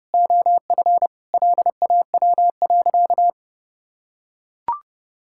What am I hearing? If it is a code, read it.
Morse code, heard as a keyed tone, about 29 wpm, 700 Hz. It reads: OF LAW.
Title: Sentences from Top 1000 Words - Review Part 5 - 30wpm